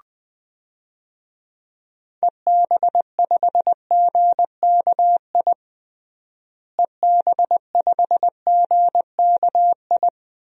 Morse code, heard as a keyed tone, about 20 wpm, 700 Hz. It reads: EB5GKI EB5GKI
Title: DXCC-100 Call Signs - 6 Characters - Repeated 2x - 20wpm